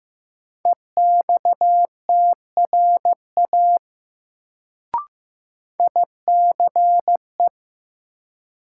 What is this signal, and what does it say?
Morse code, heard as a keyed tone, about 15 wpm, 700 Hz. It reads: EXTRA ICE
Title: Sentences from Top 1000 Words - New Word Drill - 15wpm